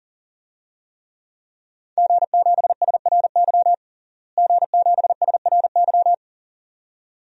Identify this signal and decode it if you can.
Morse code, heard as a keyed tone, about 40 wpm, 700 Hz. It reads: G7SRY G7SRY
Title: DXCC-100 Call Signs - 5 Characters - Repeated 2x - 40wpm